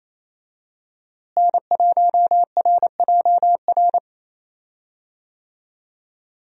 Morse code, heard as a keyed tone, about 28 wpm, 700 Hz. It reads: N1RJR
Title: USA Call Signs - 5 Characters - Repeated 3x - 28wpm